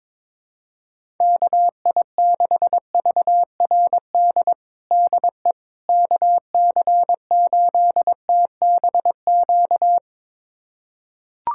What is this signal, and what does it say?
Morse code, heard as a keyed tone, about 22 wpm, 700 Hz. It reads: KI6VRD DE KC8TBQ
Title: USA Callsign DE Callsign 22wpm